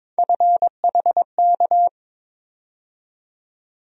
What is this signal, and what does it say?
Morse code, heard as a keyed tone, about 22 wpm, 700 Hz. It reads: FHK
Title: USA Call Sign Postfixes - 3 Letters 22wpm